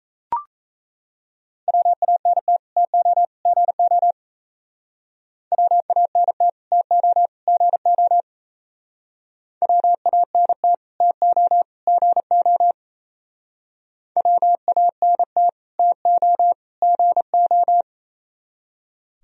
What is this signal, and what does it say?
Morse code, heard as a keyed tone, about 37 wpm, 700 Hz. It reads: WANT TO GO WANT TO GO WANT TO GO WANT TO GO